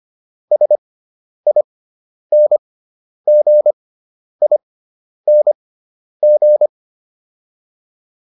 Morse code, heard as a keyed tone, about 25 wpm, spaced as for 10 wpm, 600 Hz. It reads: SINGING